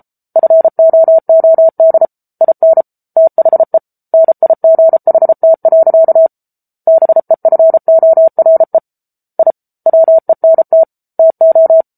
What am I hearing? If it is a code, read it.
Morse code, heard as a keyed tone, about 34 wpm, 650 Hz. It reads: FOOD IN THE NIGHT. BEFORE I WENT TO